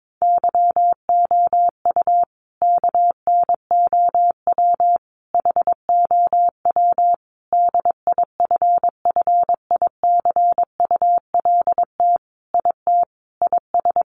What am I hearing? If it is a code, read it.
Morse code, heard as a keyed tone, about 22 wpm, 700 Hz. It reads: YOU KNOW HOW DIFFICULT IT IS
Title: Sentences from Top 1100 Words 22wpm